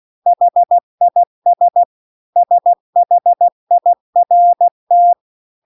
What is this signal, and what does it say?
Morse code, heard as a keyed tone, about 16 wpm, 700 Hz. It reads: HIS SHIRT
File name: CW-The-Big-Twoheaded-River-2-16WPM